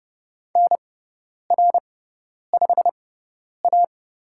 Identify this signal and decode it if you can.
Morse code, heard as a keyed tone, about 30 wpm, spaced as for 10 wpm, 700 Hz. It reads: NR5A
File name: Lesson 18g - Letter m - Call Signs 30wpm